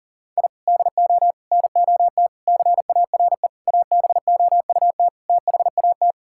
Morse code, heard as a keyed tone, about 40 wpm, 700 Hz. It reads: I DO NOT CARE ABOUT THAT
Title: Sentences from Top 1000 Words - Review Part 6 - 40wpm